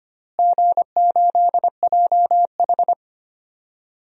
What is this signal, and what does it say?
Morse code, heard as a keyed tone, about 25 wpm, 700 Hz. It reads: G8JH
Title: DXCC-100 Call Signs - 4 Characters - Repeated 3x - 25wpm